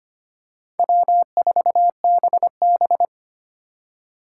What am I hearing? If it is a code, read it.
Morse code, heard as a keyed tone, about 25 wpm, 700 Hz. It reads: W4BB
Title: USA Call Signs - 4 Characters - 4x Character Spacing - 25wpm